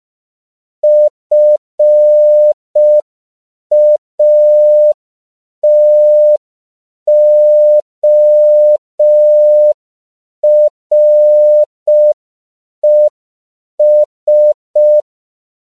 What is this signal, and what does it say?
Morse code, heard as a keyed tone, about 5 wpm, 600 Hz. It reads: FATORES